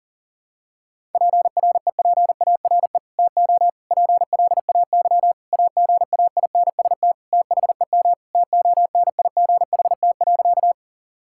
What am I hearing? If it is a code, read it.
Morse code, heard as a keyed tone, about 40 wpm, 700 Hz. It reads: PREPARE TO PLAY AGAINST THEM TONIGHT.
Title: Sentences from Top 1000 Words - Review Part 6 - 40wpm